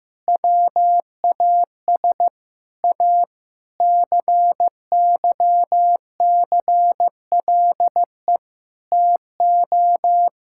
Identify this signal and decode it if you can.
Morse code, heard as a keyed tone, about 15 wpm, 700 Hz. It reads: WAS A CYCLE TO
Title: Sentences from Top 1000 Words - Review Part 6 - 15wpm